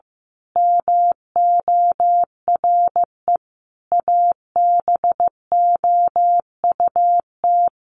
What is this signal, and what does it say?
Morse code, heard as a keyed tone, about 15 wpm, 700 Hz. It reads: MORE ABOUT
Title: Sentences from Top 200 Words 15wpm